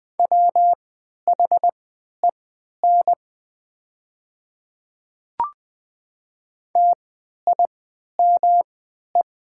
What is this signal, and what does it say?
Morse code, heard as a keyed tone, about 20 wpm, spaced as for 11 wpm, 700 Hz. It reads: WHEN TIME
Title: 3x Character Spacing - Top 100 Words - 20wpm